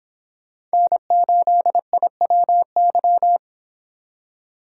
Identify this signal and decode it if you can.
Morse code, heard as a keyed tone, about 30 wpm, 700 Hz. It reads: N8IWY